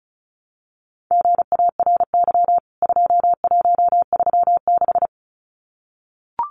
Morse code, heard as a keyed tone, about 34 wpm, 700 Hz. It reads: GARY 2136
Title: CWT Exchange 35wpm